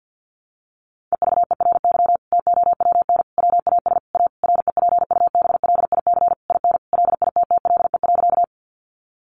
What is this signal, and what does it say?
Morse code, heard as a keyed tone, about 50 wpm, 700 Hz. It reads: EVERY TOWN WAS A REPUBLIC IN LITTLE.
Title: Sentences from Top 1500 Words 50wpm